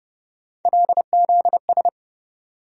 Morse code, heard as a keyed tone, about 30 wpm, 700 Hz. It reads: LZS
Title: USA Call Sign Postfixes - 3 Letters 30wpm